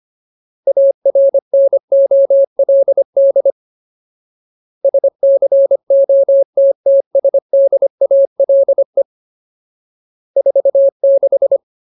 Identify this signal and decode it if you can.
Morse code, heard as a keyed tone, about 25 wpm, 550 Hz. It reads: ARNOLD SCOTTSDALE 46
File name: Ultimate Word List / 25WPM Ultimate